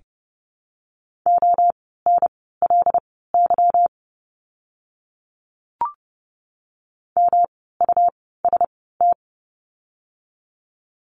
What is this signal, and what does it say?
Morse code, heard as a keyed tone, about 30 wpm, spaced as for 17 wpm, 700 Hz. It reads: ONLY MUST
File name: course / 30:10wpm / 237 Lesson 38f - Letter q - Words 30wpm